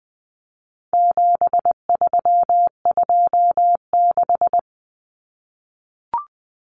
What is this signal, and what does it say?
Morse code, heard as a keyed tone, about 20 wpm, 700 Hz. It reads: 7326
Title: Common Cut Numbers - 4-Digit 20wpm